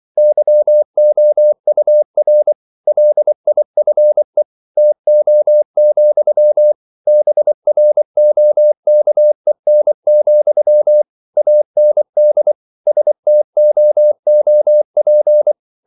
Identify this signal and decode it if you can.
Morse code, heard as a keyed tone, about 24 wpm, 600 Hz. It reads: YOUR LIFE TO, BROKEN, AND STOOP
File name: CW / Practice / If24